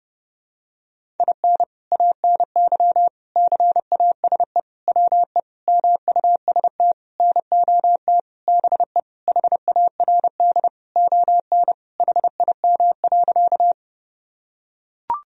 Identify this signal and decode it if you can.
Morse code, heard as a keyed tone, about 30 wpm, 700 Hz. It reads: IN ANY CASE WE MUST NOT BE HARD ON HIM.
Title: Sentences from Top 1000 Words - Review Part 1 - 30wpm